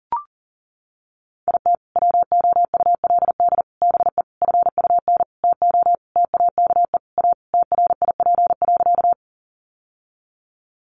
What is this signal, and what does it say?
Morse code, heard as a keyed tone, about 40 wpm, 700 Hz. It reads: IT WOULD BE FUN TO TAKE A TRIP.
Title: Sentences from Top 1300 Words 40wpm